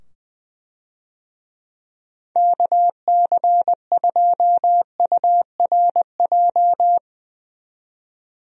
Morse code, heard as a keyed tone, about 21 wpm, 700 Hz. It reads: KC2URJ
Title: USA Call Signs - 6 Characters - 20wpm